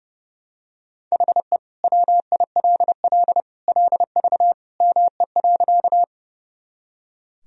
Morse code, heard as a keyed tone, about 30 wpm, 700 Hz. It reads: HE WILL LV ME.
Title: Sentences from Top 400 Words and Abbreviations 30wpm